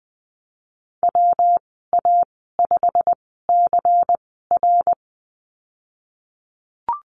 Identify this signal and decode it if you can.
Morse code, heard as a keyed tone, about 20 wpm, spaced as for 15 wpm, 700 Hz. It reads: WA5CR
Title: 140 Lesson 23g - Letter f - Call Signs 20wpm